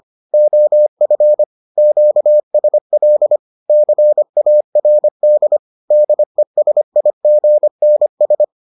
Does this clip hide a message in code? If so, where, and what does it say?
Yes: Morse code from 0.3 to 8.4 s, heard as a keyed tone, about 25 wpm, 600 Hz. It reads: OF QSL CARD DESIGNS